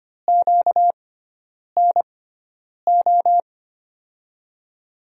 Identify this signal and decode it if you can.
Morse code, heard as a keyed tone, about 25 wpm, spaced as for 9 wpm, 700 Hz. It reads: QNO
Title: USA Call Signs - 5 Characters - 6x Character Spacing - 25wpm